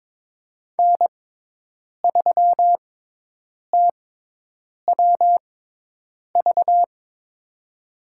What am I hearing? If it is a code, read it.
Morse code, heard as a keyed tone, about 22 wpm, spaced as for 8 wpm, 700 Hz. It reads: N3TWV
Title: USA Call Signs - 5 Characters - 6x Character Spacing - 22wpm